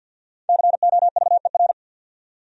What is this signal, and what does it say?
Morse code, heard as a keyed tone, about 66 wpm, spaced as for 58 wpm, 700 Hz. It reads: COVER
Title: Speed-Racing Top 300 Words 50wpm